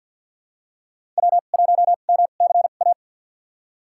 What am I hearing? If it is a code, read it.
Morse code, heard as a keyed tone, about 52 wpm, spaced as for 36 wpm, 700 Hz. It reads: W1MCA
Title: USA Call Signs - 5 Characters - 2x Character Spacing - 50wpm